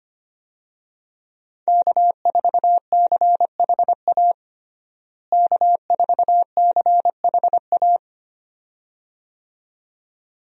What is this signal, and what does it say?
Morse code, heard as a keyed tone, about 25 wpm, 700 Hz. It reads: K4CHA K4CHA
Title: USA Call Signs - 5 Characters - Repeated 2x - 25wpm